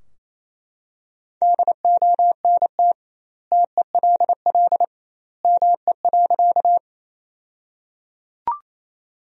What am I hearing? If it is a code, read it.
Morse code, heard as a keyed tone, about 27 wpm, 700 Hz. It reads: DONT TELL ME.
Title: 2x Word-Spacing - Sentences from Top 300 Words - 28wpm